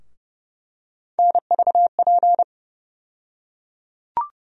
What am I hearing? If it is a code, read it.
Morse code, heard as a keyed tone, about 32 wpm, 700 Hz. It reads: NVP
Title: USA Call Sign Postfixes - 3 Letters 30wpm